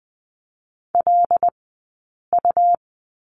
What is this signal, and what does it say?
Morse code, heard as a keyed tone, about 20 wpm, spaced as for 8 wpm, 700 Hz. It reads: LU
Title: 165 Lesson 27d - Number 7 - Two Characters 20wpm